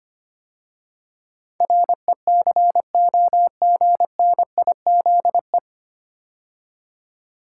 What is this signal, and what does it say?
Morse code, heard as a keyed tone, about 25 wpm, 700 Hz. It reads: RECOGNIZE
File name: practice / 25wpm / 9 Letter Words - First 3 Letters Given - 25wpm